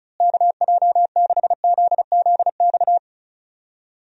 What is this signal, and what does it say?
Morse code, heard as a keyed tone, about 36 wpm, 700 Hz. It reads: KJ6ZZX
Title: USA Call Signs - 6 Characters - Repeated 2x - 35wpm